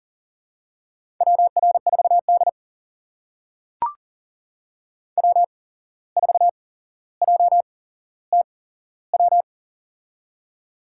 Morse code, heard as a keyed tone, about 40 wpm, spaced as for 16 wpm, 700 Hz. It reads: WR4D W4JTW
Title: USA Call Signs - 8x Character Spacing - 40wpm